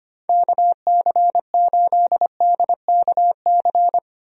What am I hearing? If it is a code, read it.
Morse code, heard as a keyed tone, about 25 wpm, 700 Hz. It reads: KC8DKC
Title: USA Call Signs - 6 Characters - Repeated 3x - 25wpm